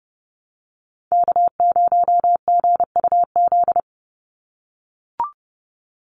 Morse code, heard as a keyed tone, about 30 wpm, 700 Hz. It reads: K0GUZ
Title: Call Signs - 30wpm